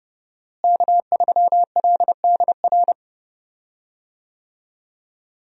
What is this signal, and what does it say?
Morse code, heard as a keyed tone, about 31 wpm, 700 Hz. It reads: K3LDR
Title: USA Call Signs - 5 Characters - Repeated 3x - 30wpm